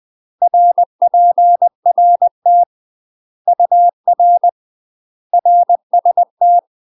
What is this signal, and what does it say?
Morse code, heard as a keyed tone, about 20 wpm, 700 Hz. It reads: RPRT UR RST
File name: UK MASTER QSOs TRAINING / 20WPM / UK MASTER QSOs 0068